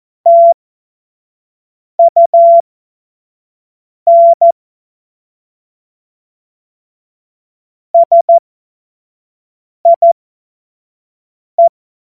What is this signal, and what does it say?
Morse code, heard as a keyed tone, about 14 wpm, spaced as for 5 wpm, 680 Hz. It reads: TUN SIE
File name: Morsekurs / Witze / Witz 25. Januar 2022, 25 F70